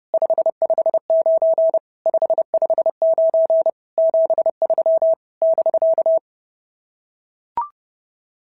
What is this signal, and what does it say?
Morse code, heard as a keyed tone, about 30 wpm, 650 Hz. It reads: 559 559 73 <BK>
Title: SOTA QSO Practice 30wpm